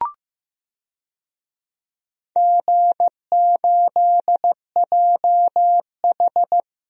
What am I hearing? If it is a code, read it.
Morse code, heard as a keyed tone, about 15 wpm, 700 Hz. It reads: G8JH
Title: DXCC-100 Call Signs - 4 Characters - Repeated 3x - 15wpm